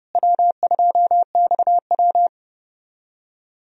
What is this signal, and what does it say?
Morse code, heard as a keyed tone, about 30 wpm, 700 Hz. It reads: W2XW
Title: USA Call Signs - 4 Characters - Repeated 3x - 30wpm